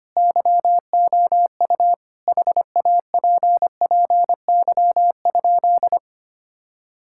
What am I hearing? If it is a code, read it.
Morse code, heard as a keyed tone, about 25 wpm, 700 Hz. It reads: YOU HAPPY?
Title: Sentences from Top 1000 Words - Review Part 5 - 25wpm